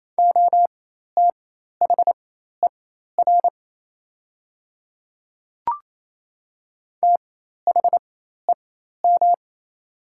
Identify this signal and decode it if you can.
Morse code, heard as a keyed tone, about 27 wpm, spaced as for 13 wpm, 700 Hz. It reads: OTHER THEM
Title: 4x Character Spacing - Top 300 Words - 28wpm